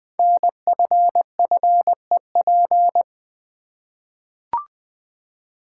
Morse code, heard as a keyed tone, about 20 wpm, 700 Hz. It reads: NFFEP
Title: Code Groups - 5 Letters - 20wpm